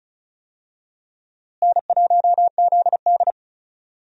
Morse code, heard as a keyed tone, about 34 wpm, 700 Hz. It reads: N1ZD
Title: USA Call Signs - 4 Characters - Repeated 2x - 35wpm